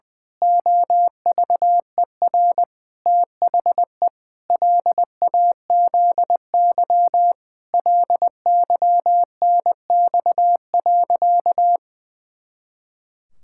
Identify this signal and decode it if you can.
Morse code, heard as a keyed tone, about 19 wpm, 700 Hz. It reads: OVER THE LAZY LYNX.
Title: Pangram Challenge v2 - 20wpm